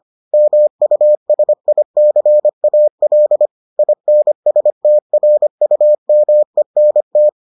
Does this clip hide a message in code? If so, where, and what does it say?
Yes: Morse code from 0.3 to 7.3 s, heard as a keyed tone, about 25 wpm, 600 Hz. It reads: MUSICAL INSTRUMENT